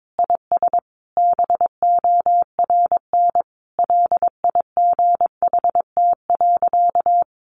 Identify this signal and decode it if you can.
Morse code, heard as a keyed tone, about 22 wpm, 700 Hz. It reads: IS BORN LIGHT.
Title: Sentences from Top 1100 Words 22wpm